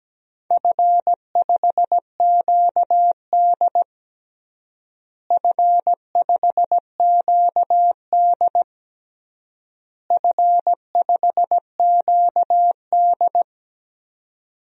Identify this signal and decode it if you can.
Morse code, heard as a keyed tone, about 17 wpm, 700 Hz. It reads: F5QD F5QD F5QD
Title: DXCC-100 Call Signs - 4 Characters - Repeated 3x - 17wpm